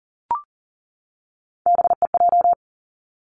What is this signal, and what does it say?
Morse code, heard as a keyed tone, about 39 wpm, 700 Hz. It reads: BEJ
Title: Three Challenging Letter-Number Combinations 40wpm